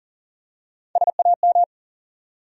Morse code, heard as a keyed tone, about 40 wpm, 700 Hz. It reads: SAM
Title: USA Common Names - Max 4 Letters - 40wpm